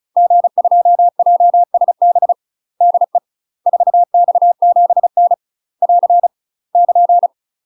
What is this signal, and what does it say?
Morse code, heard as a keyed tone, about 34 wpm, 700 Hz. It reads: G2JSB DE 4X7N <AR> <KN>
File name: UK MASTER QSOs TRAINING/35WPM